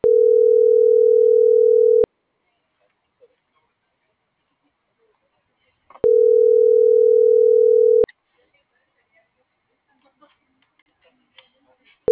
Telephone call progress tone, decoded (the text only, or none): ringback tone